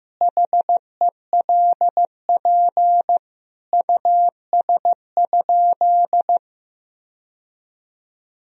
Morse code, heard as a keyed tone, about 15 wpm, 700 Hz. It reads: HELP US?
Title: Sentences from Top 1000 Words - Review Part 3 - 15wpm